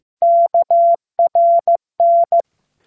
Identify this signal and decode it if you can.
Morse code, heard as a keyed tone, about 15 wpm, 680 Hz. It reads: KRN